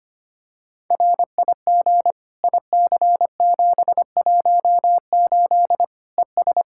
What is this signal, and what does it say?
Morse code, heard as a keyed tone, about 25 wpm, 700 Hz. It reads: RIG IC718 ES
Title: Practice QSO - Exchange 2 - Info - 25wpm